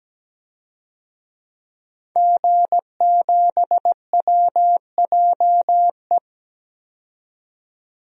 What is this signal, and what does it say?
Morse code, heard as a keyed tone, about 17 wpm, 700 Hz. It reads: G7WJE
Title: DXCC-100 Call Signs - 5 Characters - 17wpm